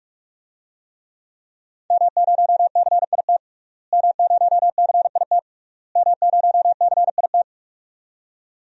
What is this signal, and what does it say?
Morse code, heard as a keyed tone, about 46 wpm, 700 Hz. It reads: M0CIT M0CIT M0CIT
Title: DXCC-100 Call Signs - 5 Characters - Repeated 3x - 45wpm